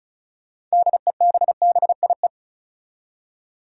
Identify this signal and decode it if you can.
Morse code, heard as a keyed tone, about 34 wpm, 700 Hz. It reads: DEBBIE
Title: USA Common Names - 35wpm